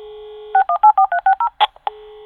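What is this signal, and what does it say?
DTMF, heard as telephone touch tones, tones of about 80 ms. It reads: 614360
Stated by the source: Sound-effects library recording of a fax machine modem signal.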